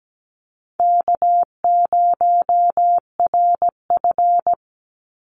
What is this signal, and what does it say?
Morse code, heard as a keyed tone, about 17 wpm, 700 Hz. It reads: K0RF
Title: Easy Call Signs Repeated 3x - 17wpm